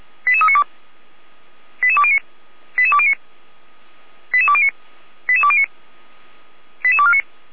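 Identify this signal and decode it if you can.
Five-tone selective call (ZVEI), heard as quick runs of five tones, tones of about 70 ms. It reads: 88392 88209 88209 88209 88209 88249
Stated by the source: ZVEI five-tone call 88392 received by SDR, from the Feuerwehr Orlamünde fire brigade audio server.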